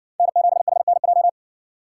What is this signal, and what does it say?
Morse code, heard as a keyed tone, about 57 wpm, 700 Hz. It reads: N7HNJ